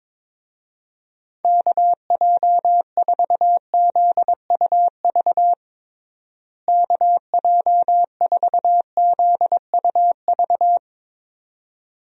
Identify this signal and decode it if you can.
Morse code, heard as a keyed tone, about 22 wpm, 700 Hz. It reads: KJ4ZUV KJ4ZUV
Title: USA Call Signs - 6 Characters - Repeated 2x - 22wpm